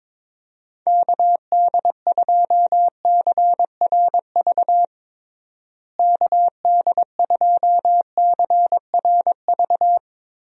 Morse code, heard as a keyed tone, about 22 wpm, 700 Hz. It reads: KD2CRV KD2CRV
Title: USA Call Signs - 6 Characters - Repeated 2x - 22wpm